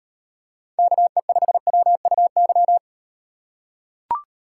Morse code, heard as a keyed tone, about 38 wpm, 700 Hz. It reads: KE5WUY